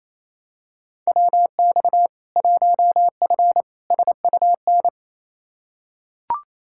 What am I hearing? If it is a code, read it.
Morse code, heard as a keyed tone, about 27 wpm, 700 Hz. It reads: WX 1F SUN
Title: Practice QSO - Exchange 2 - Info - 28wpm